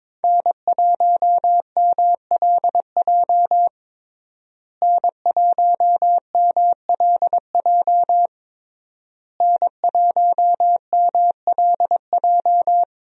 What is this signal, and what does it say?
Morse code, heard as a keyed tone, about 22 wpm, 700 Hz. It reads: N1MLJ N1MLJ N1MLJ
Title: USA Call Signs - 5 Characters - Repeated 3x - 22wpm